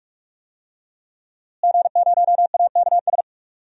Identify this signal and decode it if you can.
Morse code, heard as a keyed tone, about 46 wpm, 700 Hz. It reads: G0AKS
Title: DXCC-100 Call Signs - 5 Characters - Repeated 2x - 45wpm